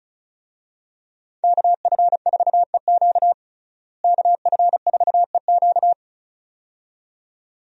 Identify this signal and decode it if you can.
Morse code, heard as a keyed tone, about 34 wpm, 700 Hz. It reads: KF4EQ KF4EQ
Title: USA Call Signs - 5 Characters - Repeated 2x - 35wpm